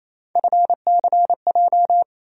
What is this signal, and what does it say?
Morse code, heard as a keyed tone, about 28 wpm, 700 Hz. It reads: FCJ
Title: USA Call Sign Postfixes - 3 Letters 28wpm